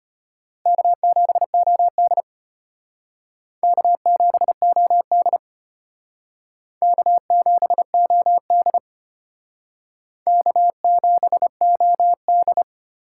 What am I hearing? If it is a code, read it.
Morse code, heard as a keyed tone, about 31 wpm, 700 Hz. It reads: K7OD K7OD K7OD K7OD
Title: Speed Racing - USA 1x2 Call Signs 25wpm